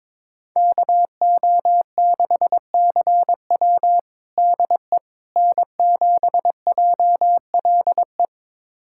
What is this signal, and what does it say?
Morse code, heard as a keyed tone, about 22 wpm, 700 Hz. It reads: KO6CW DE N7JLE
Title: USA Callsign DE Callsign - 5 Characters - 22wpm